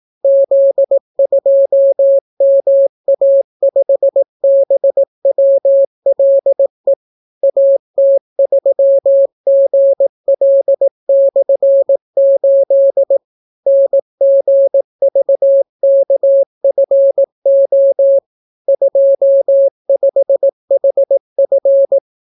Morse code, heard as a keyed tone, about 18 wpm, 550 Hz. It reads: Z2MA5BWLE AT3GL/8 NGVKFO 25HF